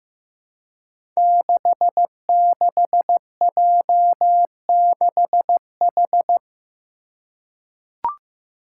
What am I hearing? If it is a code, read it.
Morse code, heard as a keyed tone, about 15 wpm, 700 Hz. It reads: 66J6H
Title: Five Challenging Letter-Number Combinations 15wpm